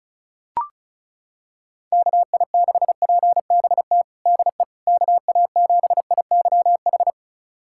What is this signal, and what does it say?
Morse code, heard as a keyed tone, about 36 wpm, 700 Hz. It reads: KI6PBT DE KA7IYH
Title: USA Callsign DE Callsign - 6 Characters - 35wpm